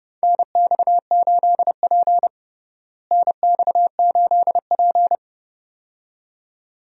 Morse code, heard as a keyed tone, about 30 wpm, 700 Hz. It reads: NX8P NX8P
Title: USA Call Signs - 4 Characters - Repeated 2x - 30wpm